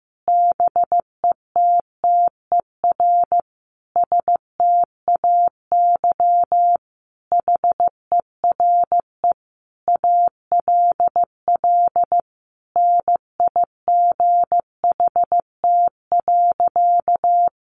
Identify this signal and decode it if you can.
Morse code, heard as a keyed tone, about 15 wpm, 700 Hz. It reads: BETTER STAY HERE ALL NIGHT.
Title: Sentences from Top 500 Words 15wpm